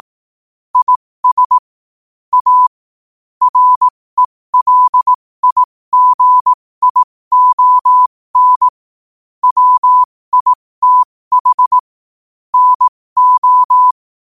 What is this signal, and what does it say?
Morse code, heard as a keyed tone, about 18 wpm, spaced as for 15 wpm, 1000 Hz. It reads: IS A RELIGION WITH NO
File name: qotdfiles / 15wpm